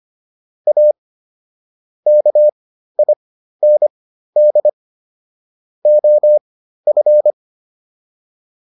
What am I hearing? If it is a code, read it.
Morse code, heard as a keyed tone, about 25 wpm, spaced as for 13 wpm, 600 Hz. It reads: A KIND OF